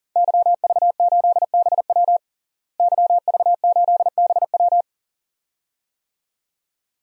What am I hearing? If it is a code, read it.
Morse code, heard as a keyed tone, about 40 wpm, 700 Hz. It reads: YV8BW YV8BW
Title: DXCC-100 Call Signs - 5 Characters - Repeated 2x - 40wpm